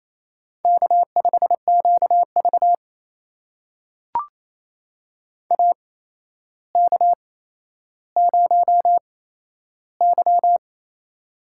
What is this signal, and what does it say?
Morse code, heard as a keyed tone, about 28 wpm, spaced as for 12 wpm, 700 Hz. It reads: K5QV AK0Y
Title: USA Call Signs - 4 Characters - 8x Character Spacing - 28wpm